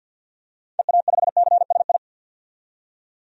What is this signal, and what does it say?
Morse code, heard as a keyed tone, about 51 wpm, 700 Hz. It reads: EA5CSI
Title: DXCC-100 Call Signs - 6 Characters - 50wpm